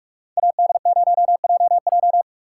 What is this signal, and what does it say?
Morse code, heard as a keyed tone, about 46 wpm, 700 Hz. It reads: AD0JJ